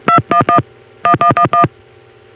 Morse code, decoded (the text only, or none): none